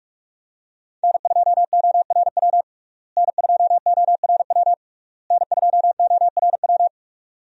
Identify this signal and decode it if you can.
Morse code, heard as a keyed tone, about 47 wpm, 700 Hz. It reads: N2ORW N2ORW N2ORW
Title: USA Call Signs - 5 Characters - Repeated 3x - 45wpm